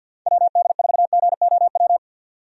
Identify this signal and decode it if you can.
Morse code, heard as a keyed tone, about 51 wpm, 700 Hz. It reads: WD4GOW
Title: USA Call Signs - 6 Characters - 50wpm